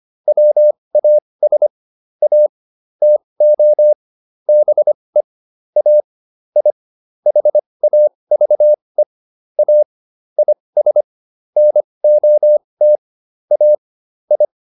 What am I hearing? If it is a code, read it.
Morse code, heard as a keyed tone, about 25 wpm, spaced as for 20 wpm, 600 Hz. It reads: WAS A TO BE A I HAVE A IS NOT A I